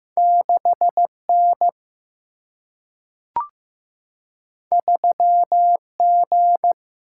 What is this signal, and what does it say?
Morse code, heard as a keyed tone, about 15 wpm, 700 Hz. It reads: 6N 3G
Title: Single-Letter Single-Number 15wpm